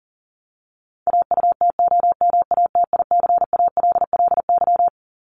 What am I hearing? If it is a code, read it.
Morse code, heard as a keyed tone, about 40 wpm, 700 Hz. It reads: AUTOMATICALLY
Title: Top 1751-2000 Words 40wpm